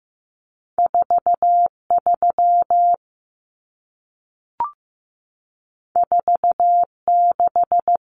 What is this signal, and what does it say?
Morse code, heard as a keyed tone, about 15 wpm, 700 Hz. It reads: 4 3 4 6
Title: Common Cut Numbers - 2-Digit 15wpm